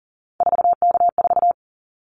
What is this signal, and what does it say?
Morse code, heard as a keyed tone, about 40 wpm, 690 Hz. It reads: 4K4
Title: Callsign Prefixes for Top 100 DXCC - 40wpm